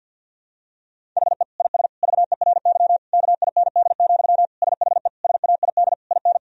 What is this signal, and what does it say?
Morse code, heard as a keyed tone, about 47 wpm, 700 Hz. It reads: HE IS VERY KIND, SHE SAID IN
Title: Sentences from Top 1000 Words - Review Part 1 - 50wpm